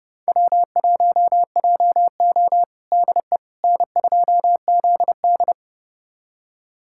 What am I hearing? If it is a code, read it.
Morse code, heard as a keyed tone, about 30 wpm, 700 Hz. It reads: W1JO DE N2ZD
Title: USA Callsign DE Callsign - 4 Characters - 30wpm